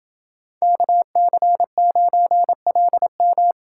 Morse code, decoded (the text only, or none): KC9LM